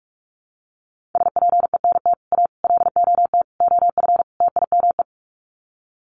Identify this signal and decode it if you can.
Morse code, heard as a keyed tone, about 44 wpm, 700 Hz. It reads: SPENT A LOT OF TIME
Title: Sets of 5 Words 45wpm